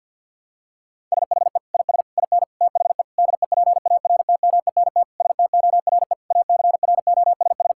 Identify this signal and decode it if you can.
Morse code, heard as a keyed tone, about 50 wpm, 700 Hz. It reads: SHE IS IN THE DEPARTMENT STORE ACROSS
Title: Sentences from Top 1000 Words - Review Part 5 - 50wpm